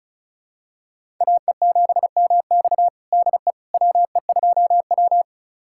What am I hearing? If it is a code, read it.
Morse code, heard as a keyed tone, about 36 wpm, 700 Hz. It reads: AE7MX DE WE2W
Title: USA Callsign DE Callsign 35wpm